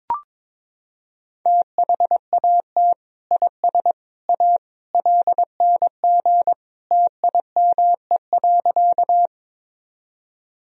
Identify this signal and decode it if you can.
Morse code, heard as a keyed tone, about 21 wpm, 700 Hz. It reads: THAT IS A LNG TIME.
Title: Sentences from Top 100 Words and Abbreviations 22wpm